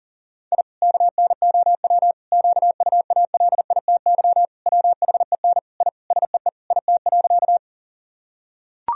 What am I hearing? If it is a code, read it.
Morse code, heard as a keyed tone, about 40 wpm, 700 Hz. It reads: I KNOW QUALITY WHEN I SEE IT.
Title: Sentences from Top 1000 Words - Review Part 4 - 40wpm